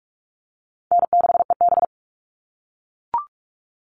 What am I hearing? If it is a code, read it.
Morse code, heard as a keyed tone, about 44 wpm, 700 Hz. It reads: N6EB
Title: All Cut Numbers - 4-Digit 45wpm